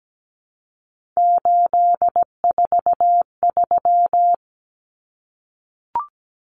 Morse code, heard as a keyed tone, about 17 wpm, 700 Hz. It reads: 843
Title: Common Cut Numbers - 3-Digit 17wpm